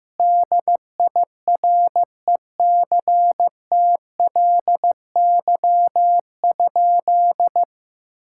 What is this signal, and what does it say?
Morse code, heard as a keyed tone, about 15 wpm, 700 Hz. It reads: DIRECTLY?
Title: Sentences from Top 1000 Words - Review Part 5 - 15wpm